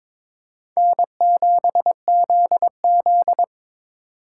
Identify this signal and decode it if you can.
Morse code, heard as a keyed tone, about 22 wpm, 700 Hz. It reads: N7ZZ
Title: USA Call Signs - 4 Characters - 3x Character Spacing - 22wpm